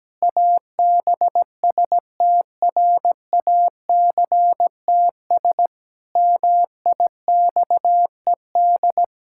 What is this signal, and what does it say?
Morse code, heard as a keyed tone, about 17 wpm, 700 Hz. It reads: ABSTRACTS MIXED
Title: Pangram Challenge v2 - 17wpm